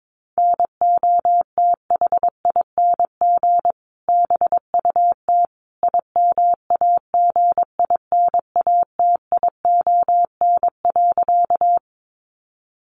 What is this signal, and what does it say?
Morse code, heard as a keyed tone, about 22 wpm, 700 Hz. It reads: NOTHING BUT IMAGINATION.
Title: Sentences from Top 1300 Words 22wpm